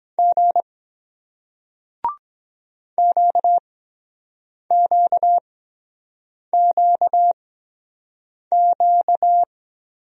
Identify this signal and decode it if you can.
Morse code, heard as a keyed tone, about 22 wpm, spaced as for 7 wpm, 700 Hz. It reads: G QQQQ